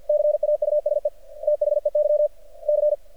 Morse code, heard as a keyed tone, about 53 wpm, 600 Hz. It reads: CAUSE THEY C
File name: archived / CRSnet